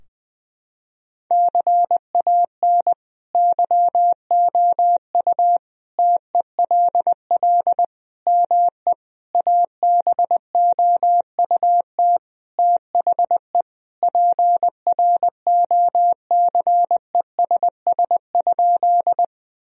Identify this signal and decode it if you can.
Morse code, heard as a keyed tone, about 21 wpm, 700 Hz. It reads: CAN YOU TELL ME ABOUT THE PROCESS?
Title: Sentences from Top 1000 Words - Review Part 3 - 20wpm